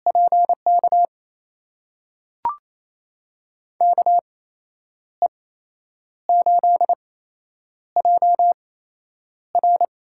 Morse code, heard as a keyed tone, about 28 wpm, spaced as for 9 wpm, 700 Hz. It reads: PK KE8JR
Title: USA Call Signs - 8x Character Spacing - 28wpm